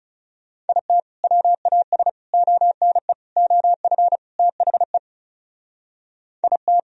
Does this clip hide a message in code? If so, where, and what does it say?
Yes: Morse code from 0.7 to 6.8 s, heard as a keyed tone, about 40 wpm, 700 Hz. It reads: IT WAS ONE OF THE IT